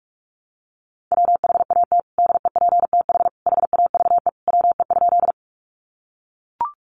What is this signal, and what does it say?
Morse code, heard as a keyed tone, about 44 wpm, 700 Hz. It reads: WHAT DEPTH HAVE WE?
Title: Sentences from Top 1200 Words 45wpm